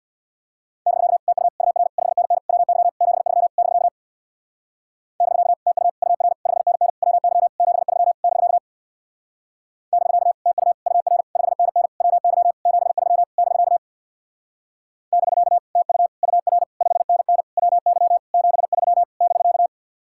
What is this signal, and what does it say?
Morse code, heard as a keyed tone, about 59 wpm, 700 Hz. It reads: <BK> TU UR 5NN WY 73 <BK> <BK> TU UR 5NN WY 73 <BK> <BK> TU UR 5NN WY 73 <BK> <BK> TU UR 5NN WY 73 <BK>
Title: Speed-Racing Common QSO Phrases - 50wpm